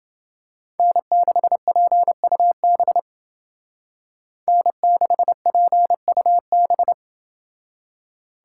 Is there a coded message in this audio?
Yes: Morse code from 0.8 to 6.9 s, heard as a keyed tone, about 28 wpm, 700 Hz. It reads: N6PUB N6PUB